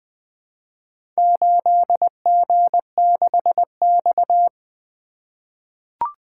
Morse code, heard as a keyed tone, about 20 wpm, 700 Hz. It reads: 8G6X
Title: Four Letter-Number Combinations - 20wpm